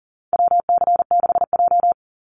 Morse code, heard as a keyed tone, about 40 wpm, 700 Hz. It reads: WC6J
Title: NA SOTA Chaser Callsigns 40wpm